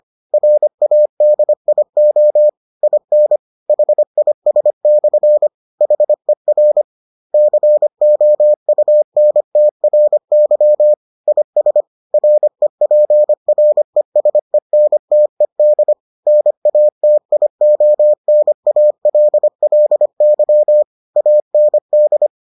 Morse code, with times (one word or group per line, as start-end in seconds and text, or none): 0.3-2.5 RADIO
2.8-3.4 IN
3.7-5.5 HIS/
5.8-6.8 HER
7.3-10.9 COUNTRY
11.3-11.8 IS
12.1-15.9 REPRESENTED
16.3-20.8 NATIONALLY
21.2-22.3 AND